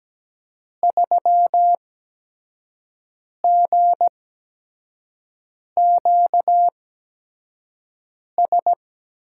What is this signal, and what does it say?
Morse code, heard as a keyed tone, about 17 wpm, spaced as for 5 wpm, 700 Hz. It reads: 3GQS
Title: USA Call Signs - 5 Characters - 8x Character Spacing - 17wpm